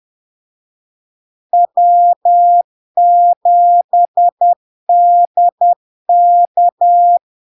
Morse code, heard as a keyed tone, about 10 wpm, 700 Hz. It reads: W7DK
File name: LARK-FD-2022-Log-in-CW-10wpm-700-Hz